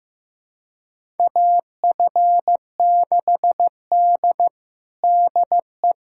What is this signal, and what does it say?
Morse code, heard as a keyed tone, about 15 wpm, 700 Hz. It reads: AF6D DE
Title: USA Callsign DE Callsign - 4 Characters - 15wpm